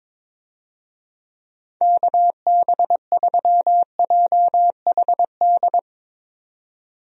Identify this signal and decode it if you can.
Morse code, heard as a keyed tone, about 22 wpm, 700 Hz. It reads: KB3JHD
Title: USA Call Signs - 6 Characters - 22wpm